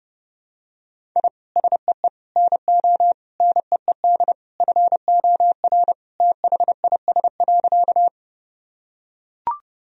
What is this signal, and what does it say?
Morse code, heard as a keyed tone, about 32 wpm, 700 Hz. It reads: I SEE NO NEED FOR THIS.
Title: Sentences from Top 1000 Words - Review Part 1 - 30wpm